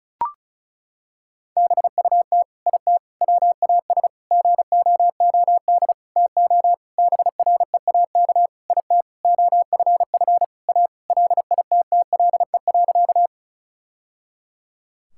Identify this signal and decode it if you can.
Morse code, heard as a keyed tone, about 34 wpm, 700 Hz. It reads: BUT IT WAS GOOD TO BREAK IT OFF A LITTLE.
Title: Sentences from Top 1000 Words - Review Part 6 - 35wpm